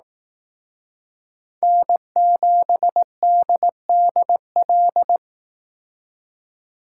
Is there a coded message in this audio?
Yes: Morse code from 1.6 to 5.2 s, heard as a keyed tone, about 19 wpm, 700 Hz. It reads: N7DDL